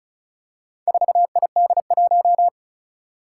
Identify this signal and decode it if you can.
Morse code, heard as a keyed tone, about 36 wpm, 700 Hz. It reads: 4ID1
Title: Four Challenging Letter-Number Combinations 35wpm